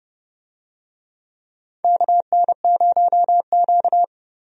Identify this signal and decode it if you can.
Morse code, heard as a keyed tone, about 30 wpm, 700 Hz. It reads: KN0Q
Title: USA Call Signs - 4 Characters - Repeated 3x - 30wpm